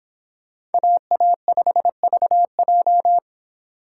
Morse code, heard as a keyed tone, about 22 wpm, 700 Hz. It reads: AA5VJ
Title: Speed Racing - USA 2x2 Call Signs 22wpm